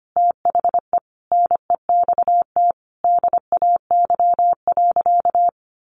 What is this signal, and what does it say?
Morse code, heard as a keyed tone, about 25 wpm, 700 Hz. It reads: THE NEXT DAY.
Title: Sentences from Top 1200 Words 25wpm